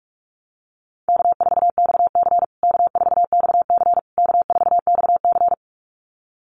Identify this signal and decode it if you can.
Morse code, heard as a keyed tone, about 44 wpm, 700 Hz. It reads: K4XC K4XC K4XC
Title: USA Call Signs Repeated 3x 45wpm